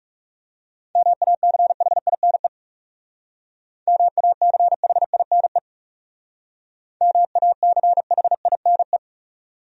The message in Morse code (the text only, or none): MACHINE MACHINE MACHINE